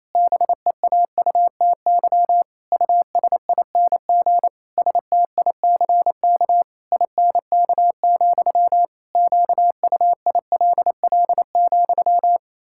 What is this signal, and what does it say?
Morse code, heard as a keyed tone, about 28 wpm, 700 Hz. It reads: BEAUTY USING STICK INK, QUILL,
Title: Pangram Challenge v2 - 28wpm